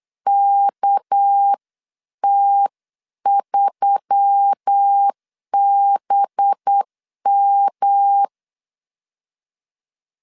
Morse code, heard as a keyed tone, about 8 wpm, 800 Hz. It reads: KT3BM
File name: 8wpm